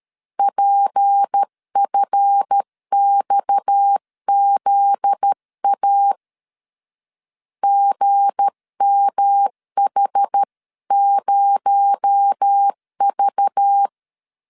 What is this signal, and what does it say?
Morse code, heard as a keyed tone, about 13 wpm, 800 Hz. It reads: PFXZA GMH0V
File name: LEZIONI / 7 GF9Z / 7 GF9Z 12wpm 2147